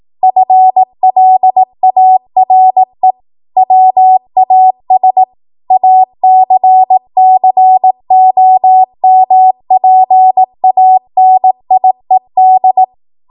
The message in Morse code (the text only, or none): FLARE WAS ACCOMPANIED